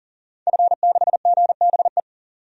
Morse code, heard as a keyed tone, about 37 wpm, 700 Hz. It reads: F6GBE
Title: DXCC-100 Call Signs - 5 Characters - 40wpm